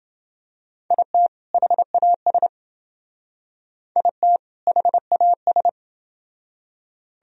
Morse code, heard as a keyed tone, about 28 wpm, 700 Hz. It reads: IT HAS IT HAS